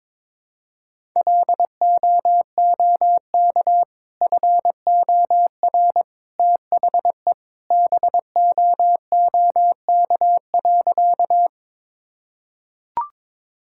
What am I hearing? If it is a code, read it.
Morse code, heard as a keyed tone, about 22 wpm, 700 Hz. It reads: LOOK FOR THE BOOK.
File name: practice / 22wpm / Sentences with 2-Word Phrases - 22wpm